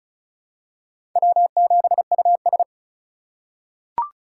Morse code, heard as a keyed tone, about 34 wpm, 700 Hz. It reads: W7US
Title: USA Call Signs - 4 Characters - 3x Character Spacing - 35wpm